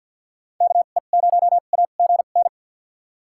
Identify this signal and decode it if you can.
Morse code, heard as a keyed tone, about 53 wpm, spaced as for 37 wpm, 700 Hz. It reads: KE0AGN
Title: USA Call Signs - 2x Character Spacing - 50wpm